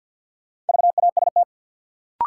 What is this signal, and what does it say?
Morse code, heard as a keyed tone, about 52 wpm, 700 Hz. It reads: VAST
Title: Sentences from Top 1300 Words - New Word Drill - 50wpm